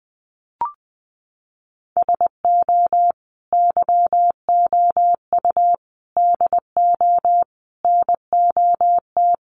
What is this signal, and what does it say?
Morse code, heard as a keyed tone, about 20 wpm, 700 Hz. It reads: SO YOU DO NOT
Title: Sets of 5 Words Encore 20wpm